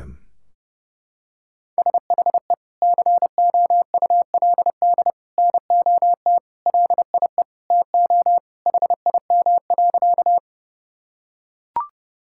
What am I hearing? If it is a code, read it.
Morse code, heard as a keyed tone, about 30 wpm, 700 Hz. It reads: SHE COULD NOT LIE TO HIM.
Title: Sentences from Top 1600 Words 30wpm